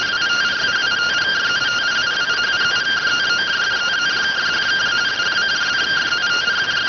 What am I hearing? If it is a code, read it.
Two-tone teleprinter signal, about 170 Hz shift.